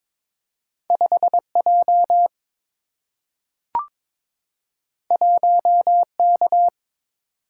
Morse code, heard as a keyed tone, about 22 wpm, 700 Hz. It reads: 5 J 1 K
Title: Single-Letter Single-Number 22wpm